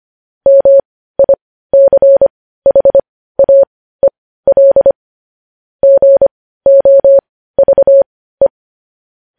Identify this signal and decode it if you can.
Morse code, heard as a keyed tone, about 25 wpm, spaced as for 15 wpm, 550 Hz. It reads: MICHAEL GOVE
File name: podcast / News headlines-1x-25x15